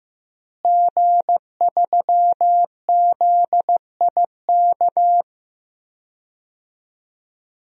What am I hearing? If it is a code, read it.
Morse code, heard as a keyed tone, about 14 wpm, 700 Hz. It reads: G3ZIK
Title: DXCC-100 Call Signs - 5 Characters - Repeated 2x - 15wpm